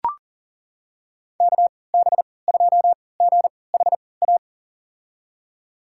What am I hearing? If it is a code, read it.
Morse code, heard as a keyed tone, about 40 wpm, spaced as for 23 wpm, 700 Hz. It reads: KB2GHA
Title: USA Call Signs - 6 Characters - 3x Character Spacing - 40wpm